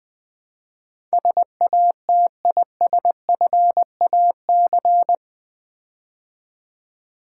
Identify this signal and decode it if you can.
Morse code, heard as a keyed tone, about 20 wpm, 700 Hz. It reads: SATISFAC
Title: Build A Word - 4 Letters At A Time - 12-Letter Words - 20wpm